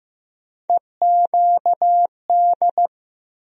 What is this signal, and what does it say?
Morse code, heard as a keyed tone, about 15 wpm, 700 Hz. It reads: EQD
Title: USA Call Sign Postfixes - 3 Letters 15wpm